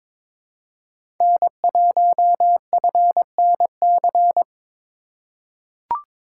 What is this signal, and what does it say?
Morse code, heard as a keyed tone, about 22 wpm, 700 Hz. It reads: N1FNC
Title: USA Call Signs - 5 Characters - 4x Character Spacing - 22wpm